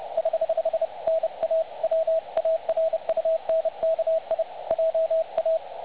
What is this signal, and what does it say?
Morse code, heard as a keyed tone, about 30 wpm, 660 Hz. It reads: <HH> NA WARUNKI JA
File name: dlazaawansowanych / nagraniazpasma / websdr recording 3565.0kHz